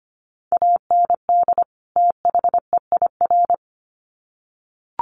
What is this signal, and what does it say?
Morse code, heard as a keyed tone, about 25 wpm, 700 Hz. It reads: AND THEIR
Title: Sets of 2 Words 25wpm